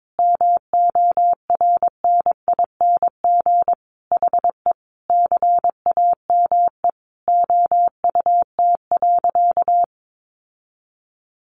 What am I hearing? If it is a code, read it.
Morse code, heard as a keyed tone, about 22 wpm, 700 Hz. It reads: MORNING HE CAME OUT.
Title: Sentences from Top 1300 Words 22wpm